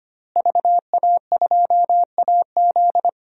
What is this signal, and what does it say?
Morse code, heard as a keyed tone, about 25 wpm, 700 Hz. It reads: VA2AZ
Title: DXCC-100 Call Signs - 5 Characters - Repeated 3x - 25wpm